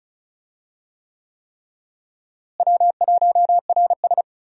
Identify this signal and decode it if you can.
Morse code, heard as a keyed tone, about 34 wpm, 700 Hz. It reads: W1RS
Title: USA Call Signs - 4 Characters - Repeated 3x - 35wpm